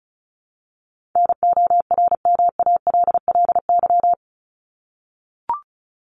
Morse code, heard as a keyed tone, about 34 wpm, 700 Hz. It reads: NORMALLY
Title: LY Suffix - Top 100 Words with Base Words 35wpm